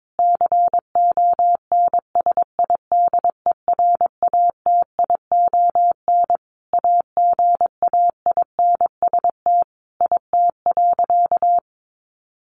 Morse code, heard as a keyed tone, about 22 wpm, 700 Hz. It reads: CONSIDERATION AGAINST IT.
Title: Sentences from Top 1500 Words 22wpm